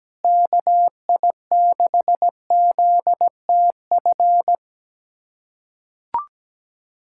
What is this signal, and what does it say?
Morse code, heard as a keyed tone, about 17 wpm, 700 Hz. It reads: KI6ZTF
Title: USA Callsign DE Callsign - 6 Characters - 17wpm